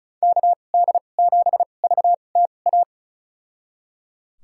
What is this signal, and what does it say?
Morse code, heard as a keyed tone, about 34 wpm, spaced as for 25 wpm, 700 Hz. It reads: KD7VTA
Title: USA Call Signs - 6 Characters - 2x Character Spacing - 35wpm